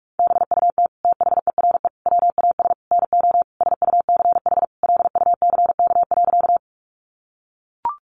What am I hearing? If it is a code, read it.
Morse code, heard as a keyed tone, about 44 wpm, 700 Hz. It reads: BUT THERE WAS NO SUCH LUCK.
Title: Sentences from Top 1500 Words 45wpm